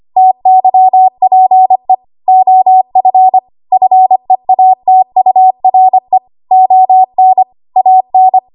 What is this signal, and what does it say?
Morse code, heard as a keyed tone, about 25 wpm, 750 Hz. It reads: TYPE OF FEATURE ON AN
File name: Archive / 25 WPM / 25WPM